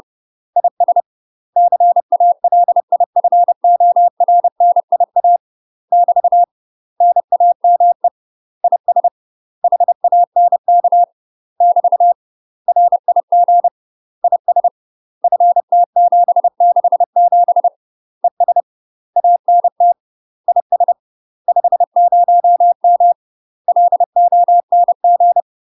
Morse code, heard as a keyed tone, about 30 wpm, 700 Hz. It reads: IS CALIFORNIA <BT> NAME IS HANK <BT> RIG IS FT767 ES ANT IS 50M LONG
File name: UK MASTER QSOs TRAINING / 30WPM / UK MASTER QSOs 0030